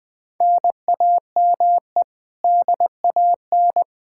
Morse code, heard as a keyed tone, about 21 wpm, 700 Hz. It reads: NAME DAN
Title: Practice QSO - Exchange 1 - Beginning - 20wpm